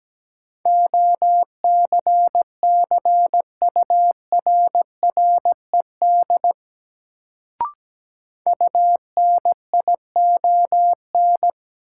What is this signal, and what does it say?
Morse code, heard as a keyed tone, about 17 wpm, 700 Hz. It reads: OCCURRED UNION
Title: Sets of 5 Words Encore - New Word Drill - 17wpm